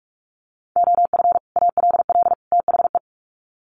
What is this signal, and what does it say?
Morse code, heard as a keyed tone, about 44 wpm, 700 Hz. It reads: OF ALL THE
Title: Sets of 3 Words 45wpm